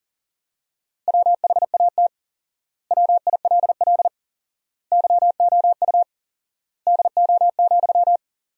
Morse code, heard as a keyed tone, about 37 wpm, 700 Hz. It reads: WHAT WILL YOU DO,
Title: 4x Word-Spacing - Sentences from Top 100 Words - 40wpm